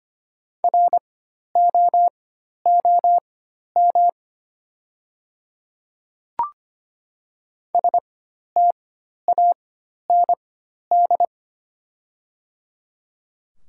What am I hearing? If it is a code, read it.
Morse code, heard as a keyed tone, about 25 wpm, spaced as for 12 wpm, 690 Hz. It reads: ROOM STAND
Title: 4x Character Spacing - Top 500 Words - 25wpm